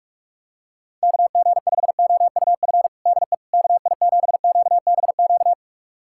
Morse code, heard as a keyed tone, about 43 wpm, 700 Hz. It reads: KG5OUF DE KI7QBQ
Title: USA Callsign DE Callsign 45wpm